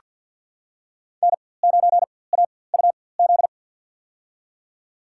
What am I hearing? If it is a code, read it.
Morse code, heard as a keyed tone, about 49 wpm, spaced as for 23 wpm, 700 Hz. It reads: N9AUZ
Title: USA Call Signs - 4x Character Spacing - 50wpm